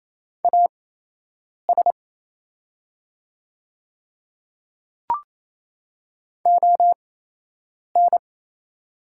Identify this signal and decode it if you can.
Morse code, heard as a keyed tone, about 28 wpm, spaced as for 8 wpm, 700 Hz. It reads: AS ON
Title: 8x Character Spacing - Top 200 Words - 28wpm